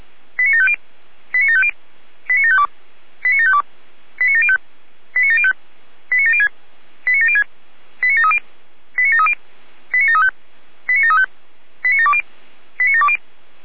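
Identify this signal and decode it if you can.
Five-tone selective call (ZVEI), heard as quick runs of five tones, tones of about 70 ms. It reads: 89750 89750 89752 89752 89705 89705 89706 89706 89730 89730 89735 89735 89720 89720
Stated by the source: ZVEI five-tone call received on SDR.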